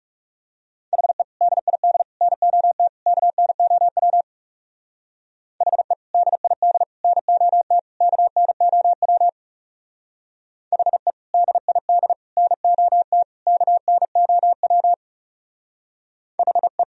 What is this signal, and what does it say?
Morse code, heard as a keyed tone, about 36 wpm, 700 Hz. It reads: HE DID NOT KNOW HE DID NOT KNOW HE DID NOT KNOW HE
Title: Speed-Racing - Sets of 4 Words 30wpm